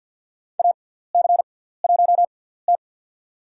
Morse code, heard as a keyed tone, about 49 wpm, spaced as for 17 wpm, 690 Hz. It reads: AC1T
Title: USA Call Signs - 6x Character Spacing - 50wpm